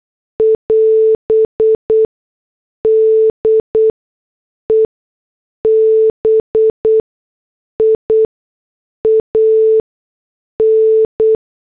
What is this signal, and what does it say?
Morse code, heard as a keyed tone, about 8 wpm, spaced as for 6 wpm, 440 Hz. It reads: <AS> DEBIAN